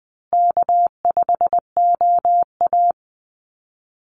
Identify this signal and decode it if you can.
Morse code, heard as a keyed tone, about 20 wpm, 700 Hz. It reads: K5OA
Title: Easy Call Signs Repeated 2x - 20wpm